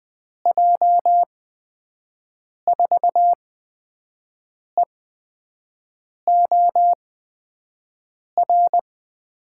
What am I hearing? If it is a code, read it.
Morse code, heard as a keyed tone, about 21 wpm, spaced as for 5 wpm, 700 Hz. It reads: J4EOR
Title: USA Call Signs - 8x Character Spacing - 20wpm